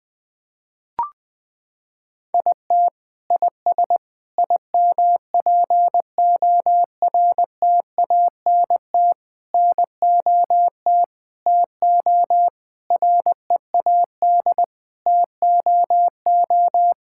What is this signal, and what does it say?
Morse code, heard as a keyed tone, about 20 wpm, 700 Hz. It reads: IT IS IMPORTANT NOT TO READ TOO
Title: Sentences with 9-Word Phrases - 20wpm